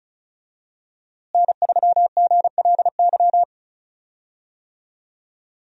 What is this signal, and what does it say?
Morse code, heard as a keyed tone, about 36 wpm, 700 Hz. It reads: N3GLY
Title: USA Call Signs - 5 Characters - Repeated 3x - 35wpm